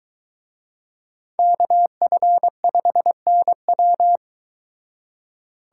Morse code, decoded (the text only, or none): KF5NW